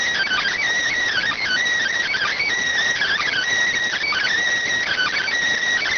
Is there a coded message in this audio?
Yes: Emergency Alert System (SAME) tones from 1.1 to 5.2 s, heard as harsh buzzing bursts.